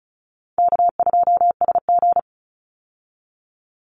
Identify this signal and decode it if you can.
Morse code, heard as a keyed tone, about 34 wpm, 700 Hz. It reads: K2SG
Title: Easy Call Signs Repeated 3x - 35wpm